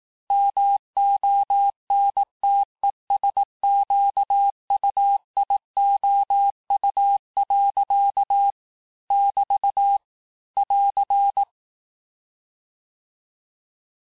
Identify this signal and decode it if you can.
Morse code, heard as a keyed tone, about 18 wpm, 800 Hz. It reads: MONTESQUIOU. <BT> <AR>
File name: cours-du-lundi files / cours-1594--teste-clair-vitesse-17mpm-puis-22mpm